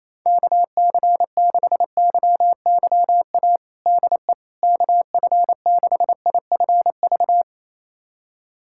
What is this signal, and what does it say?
Morse code, heard as a keyed tone, about 27 wpm, 700 Hz. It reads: KC6YYA DE KF6IFV
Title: USA Callsign DE Callsign - 6 Characters - 28wpm